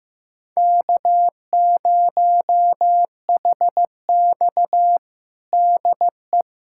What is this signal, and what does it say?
Morse code, heard as a keyed tone, about 15 wpm, 700 Hz. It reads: K0HX DE
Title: USA Callsign DE Callsign 15wpm